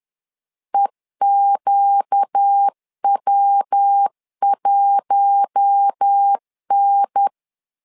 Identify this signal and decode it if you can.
Morse code, heard as a keyed tone, about 10 wpm, 800 Hz. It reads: EQW1N
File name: LEZIONI / 4 Q1 10wpm 2448